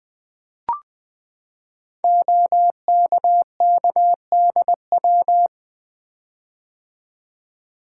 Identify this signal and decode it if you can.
Morse code, heard as a keyed tone, about 20 wpm, 700 Hz. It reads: OKKDW
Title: Code Groups - 5 Letter:Numbers - 20wpm